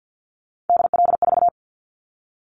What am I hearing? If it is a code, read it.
Morse code, heard as a keyed tone, about 50 wpm, 700 Hz. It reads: DL4
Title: Callsign Prefixes for Top 50 DXCC - 50wpm